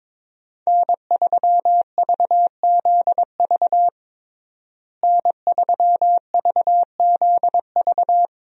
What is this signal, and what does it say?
Morse code, heard as a keyed tone, about 22 wpm, 700 Hz. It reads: N3VZV N3VZV
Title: USA Call Signs - 5 Characters - Repeated 2x - 22wpm